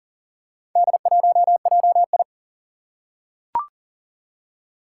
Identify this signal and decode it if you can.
Morse code, heard as a keyed tone, about 40 wpm, 700 Hz. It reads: D1JI
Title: Four Challenging Letter-Number Combinations 40wpm